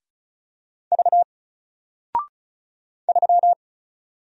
Morse code, heard as a keyed tone, about 34 wpm, 700 Hz. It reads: V 3
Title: Single Character (Letter:Number:Punctuation:Prosign) 35wpm